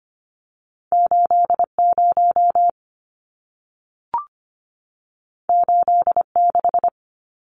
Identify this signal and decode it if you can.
Morse code, heard as a keyed tone, about 25 wpm, 700 Hz. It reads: 80 86
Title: Numbers - 2-Digit 25wpm